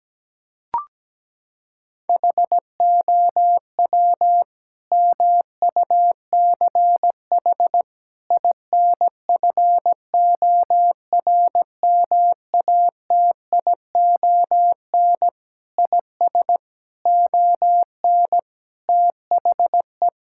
Morse code, heard as a keyed tone, about 17 wpm, 700 Hz. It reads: HOW MUCH INFORMATION IS ON THE
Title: Sentences from Top 1000 Words - Review Part 6 - 17wpm